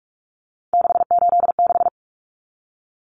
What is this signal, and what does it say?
Morse code, heard as a keyed tone, about 44 wpm, 700 Hz. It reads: 686
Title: Common Cut Numbers - 3-Digit 45wpm